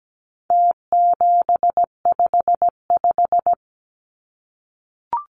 Morse code, heard as a keyed tone, about 17 wpm, 700 Hz. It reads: T755
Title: Numbers - Contest Style - 4-Digit 17wpm